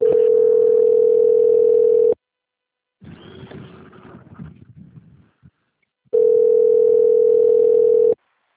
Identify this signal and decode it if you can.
Ringback tone (North America), 440 + 480 Hz.